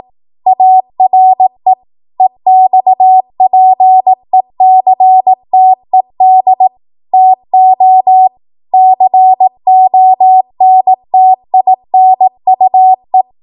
Morse code, heard as a keyed tone, about 18 wpm, 750 Hz. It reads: ARE EXPECTED TO CONTINUE
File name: Morse / 18WPM